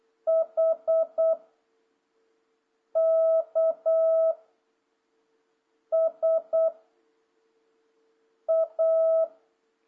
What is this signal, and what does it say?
Morse code, heard as a keyed tone, about 8 wpm, 650 Hz. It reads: H K S A
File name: jh8cbh / 2026cw